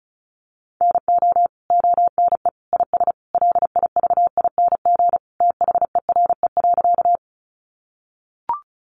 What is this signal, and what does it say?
Morse code, heard as a keyed tone, about 34 wpm, 700 Hz. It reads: NO ONE IS LIVING THERE.
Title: Sentences from Top 1100 Words 35wpm